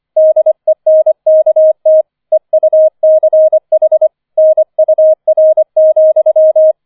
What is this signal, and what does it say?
Morse code, heard as a keyed tone, about 25 wpm, 630 Hz. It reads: DENKT EUCH NUR,